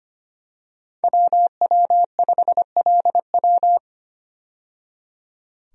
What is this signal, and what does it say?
Morse code, heard as a keyed tone, about 25 wpm, 690 Hz. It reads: WW5LW
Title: USA Call Signs - 5 Characters - 25wpm